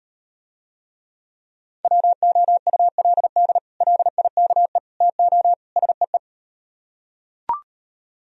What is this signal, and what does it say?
Morse code, heard as a keyed tone, about 38 wpm, 700 Hz. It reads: WOULD LIKE TO SEE